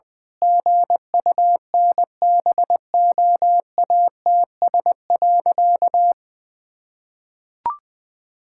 Morse code, heard as a keyed tone, about 20 wpm, 700 Hz. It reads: GUNBOATS.
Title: Pangram Challenge v2 - 20wpm